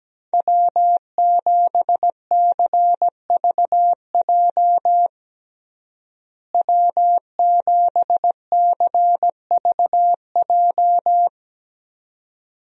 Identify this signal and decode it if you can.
Morse code, heard as a keyed tone, about 17 wpm, 700 Hz. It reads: W7CVJ W7CVJ
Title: USA Call Signs - 5 Characters - Repeated 2x - 17wpm